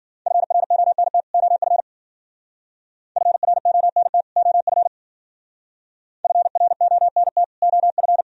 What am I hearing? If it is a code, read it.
Morse code, heard as a keyed tone, about 49 wpm, 700 Hz. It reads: FRONT OF FRONT OF FRONT OF
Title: Speed-Racing - Sets of 2 Words 40wpm